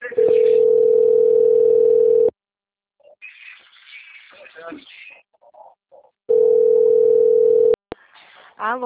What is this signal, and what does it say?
Ringback tone (North America), 440 + 480 Hz, with speech.
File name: out- - - - - .99537